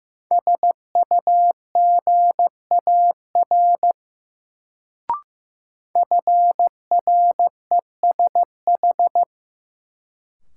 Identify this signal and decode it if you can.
Morse code, heard as a keyed tone, about 15 wpm, 700 Hz. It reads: SUGAR FRESH
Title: Sentences from Top 1200 Words - New Word Drill Alt - 15wpm